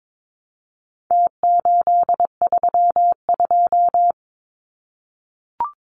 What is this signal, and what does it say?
Morse code, heard as a keyed tone, about 22 wpm, 700 Hz. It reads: T832
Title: Numbers - Contest Style - 4-Digit 22wpm